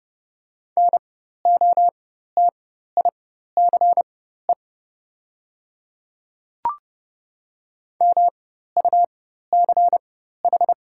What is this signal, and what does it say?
Morse code, heard as a keyed tone, about 29 wpm, spaced as for 14 wpm, 700 Hz. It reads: NOTICE MUCH
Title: 4x Character Spacing - Top 500 Words - 30wpm